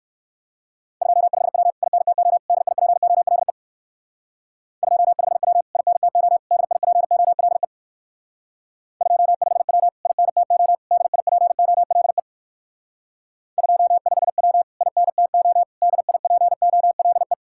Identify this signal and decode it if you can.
Morse code, heard as a keyed tone, about 56 wpm, 700 Hz. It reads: 25W INTO DIPOLE 25W INTO DIPOLE 25W INTO DIPOLE 25W INTO DIPOLE
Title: Speed-Racing Common QSO Phrases Encore - 45wpm